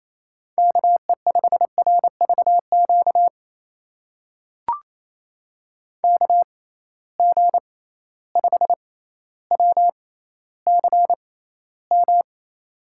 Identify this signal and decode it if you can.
Morse code, heard as a keyed tone, about 28 wpm, spaced as for 14 wpm, 700 Hz. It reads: KE5RVQ KG5WCM
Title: USA Call Signs - 6 Characters - 6x Character Spacing - 28wpm